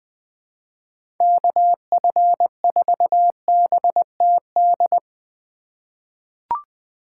Morse code, heard as a keyed tone, about 20 wpm, 700 Hz. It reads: KF4BTD
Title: USA Call Signs - 6 Characters - 4x Character Spacing - 20wpm